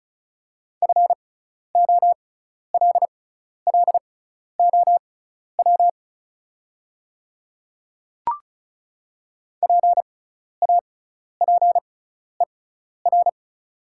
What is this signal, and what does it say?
Morse code, heard as a keyed tone, about 36 wpm, spaced as for 12 wpm, 690 Hz. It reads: FOLLOW PAPER
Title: 6x Character Spacing - Top 300 Words - 35wpm